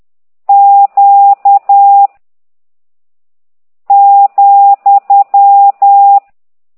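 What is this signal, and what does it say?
Morse code, heard as a keyed tone, about 10 wpm, 800 Hz. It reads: Q ,
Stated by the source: Morse code lesson 5 exercise 10 by CT2IDW, containing only characters M, O, G, Z, Q and comma.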